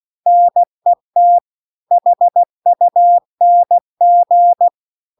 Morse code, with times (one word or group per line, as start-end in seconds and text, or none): 0.3-1.4 NET
1.9-4.7 HUNG